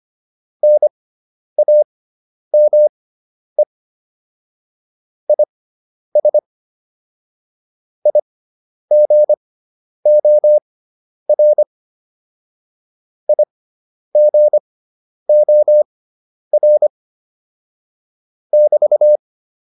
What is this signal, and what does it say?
Morse code, heard as a keyed tone, about 25 wpm, spaced as for 10 wpm, 600 Hz. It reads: NAME IS IGOR IGOR <BT>